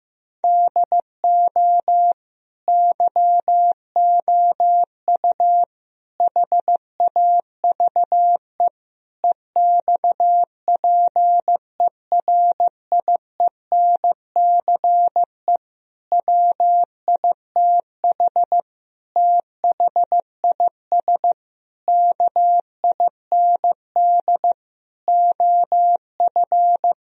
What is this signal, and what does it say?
Morse code, heard as a keyed tone, about 15 wpm, 700 Hz. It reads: DO YOU HAVE EXPERIENCE WITH THIS KIND OF
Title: Sentences from Top 1000 Words - Review Part 3 - 15wpm